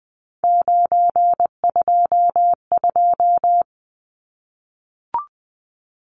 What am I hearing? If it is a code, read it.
Morse code, heard as a keyed tone, about 20 wpm, 700 Hz. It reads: 922
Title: Numbers - 3-Digit 20wpm